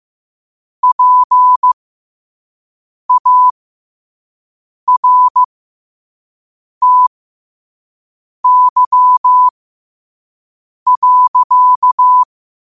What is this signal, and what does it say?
Morse code, heard as a keyed tone, about 15 wpm, spaced as for 5 wpm, 1000 Hz. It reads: PARTY.
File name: qotdfiles / 5wpm